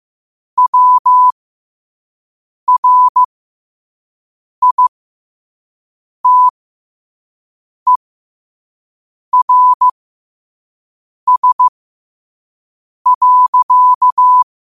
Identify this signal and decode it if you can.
Morse code, heard as a keyed tone, about 15 wpm, spaced as for 5 wpm, 1000 Hz. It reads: WRITERS.
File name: qotdfiles / 5wpm